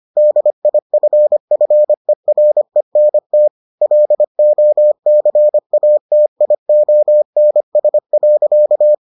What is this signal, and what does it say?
Morse code, heard as a keyed tone, about 25 wpm, 600 Hz. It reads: DIFFERENT LOCATIONS.